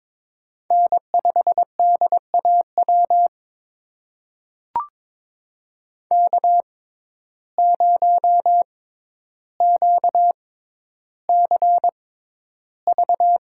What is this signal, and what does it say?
Morse code, heard as a keyed tone, about 22 wpm, spaced as for 11 wpm, 700 Hz. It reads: N5DAW K0QCV
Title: USA Call Signs - 5 Characters - 6x Character Spacing - 22wpm